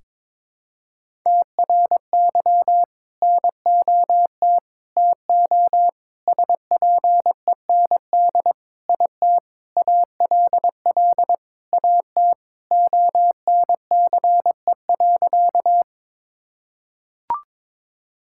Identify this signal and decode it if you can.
Morse code, heard as a keyed tone, about 22 wpm, 700 Hz. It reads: TRY NOT TO SPEND IT ALL AT ONCE.
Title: Sentences from Top 1000 Words - Review Part 5 - 22wpm